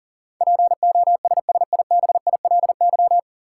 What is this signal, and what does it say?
Morse code, heard as a keyed tone, about 40 wpm, 700 Hz. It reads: POSSIBILY
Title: Sentences from Top 1300 Words - New Word Drill Alt - 40wpm